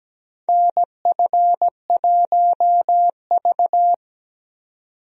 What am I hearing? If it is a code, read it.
Morse code, heard as a keyed tone, about 17 wpm, 700 Hz. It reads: NF1V
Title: USA Call Signs - 4 Characters - 17wpm